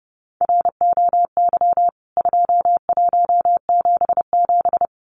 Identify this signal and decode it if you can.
Morse code, heard as a keyed tone, about 30 wpm, 700 Hz. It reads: ROY 2177
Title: CWT Exchange 30wpm